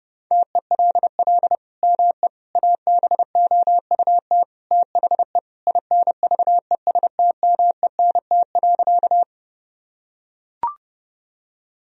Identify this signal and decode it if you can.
Morse code, heard as a keyed tone, about 30 wpm, 700 Hz. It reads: TELL ME ABOUT THE INVESTMENT.
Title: Sentences from Top 1000 Words - Review Part 4 - 30wpm